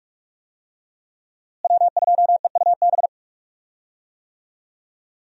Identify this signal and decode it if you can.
Morse code, heard as a keyed tone, about 46 wpm, 700 Hz. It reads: W2EUB
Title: USA Call Signs - 5 Characters - Repeated 3x - 45wpm